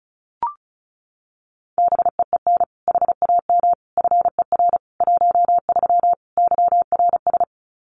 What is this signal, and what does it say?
Morse code, heard as a keyed tone, about 34 wpm, 700 Hz. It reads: BEEN HAM FER 13 YRS
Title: Common QSO Phrases 35wpm